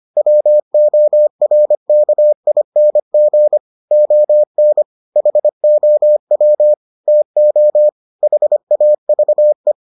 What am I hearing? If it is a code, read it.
Morse code, heard as a keyed tone, about 25 wpm, 600 Hz. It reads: WORKING ON HOW TO HAVE